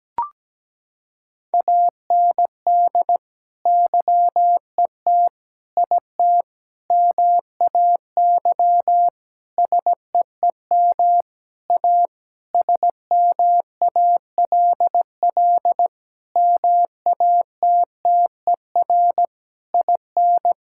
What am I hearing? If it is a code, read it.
Morse code, heard as a keyed tone, about 17 wpm, 700 Hz. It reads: AND YET IT MAY SEEM A SMALL MATTER IN
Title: Sentences from Top 1000 Words - Review Part 5 - 17wpm